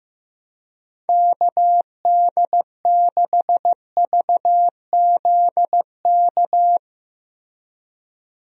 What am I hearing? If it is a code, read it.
Morse code, heard as a keyed tone, about 15 wpm, 700 Hz. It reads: KD6VZK
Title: USA Call Signs - 6 Characters - Repeated 3x - 15wpm